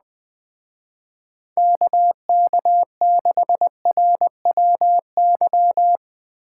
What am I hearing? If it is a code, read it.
Morse code, heard as a keyed tone, about 20 wpm, 700 Hz. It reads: KK6RWY
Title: Speed Racing - USA Call Signs 17wpm